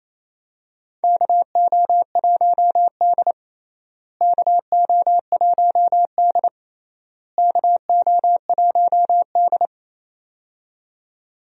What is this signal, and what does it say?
Morse code, heard as a keyed tone, about 28 wpm, 700 Hz. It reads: KO1D KO1D KO1D
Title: USA Call Signs - 4 Characters - Repeated 3x - 28wpm